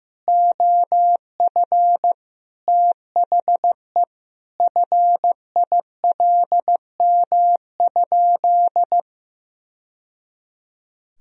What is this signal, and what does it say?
Morse code, heard as a keyed tone, about 16 wpm, 700 Hz. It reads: OF THE FILM?
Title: Sentences from Top 1000 Words - Review Part 3 - 15wpm